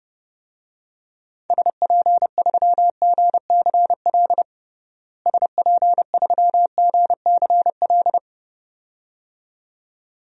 Morse code, heard as a keyed tone, about 30 wpm, 700 Hz. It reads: SP3GCL SP3GCL
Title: DXCC-100 Call Signs - 6 Characters - Repeated 2x - 30wpm